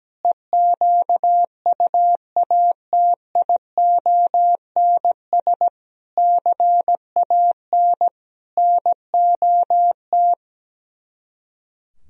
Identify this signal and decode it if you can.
Morse code, heard as a keyed tone, about 17 wpm, 700 Hz. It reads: EQUATIONS CAN NOT
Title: Speed-Racing - Sets of 5 Words 17wpm